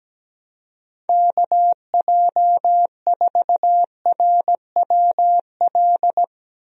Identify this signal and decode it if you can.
Morse code, heard as a keyed tone, about 17 wpm, 700 Hz. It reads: KJ4RWL
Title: USA Call Signs - 6 Characters - Repeated 3x - 17wpm